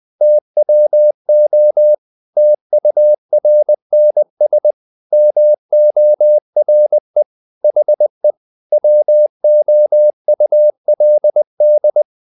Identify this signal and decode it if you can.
Morse code, heard as a keyed tone, about 20 wpm, 600 Hz. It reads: TWO TURNS MORE HE WOULD
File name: the old man and the sea part 2 20wpm